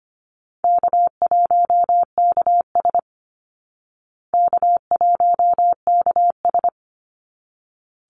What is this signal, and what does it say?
Morse code, heard as a keyed tone, about 25 wpm, 700 Hz. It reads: K1KS K1KS
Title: Easy Call Signs Repeated 2x - 25wpm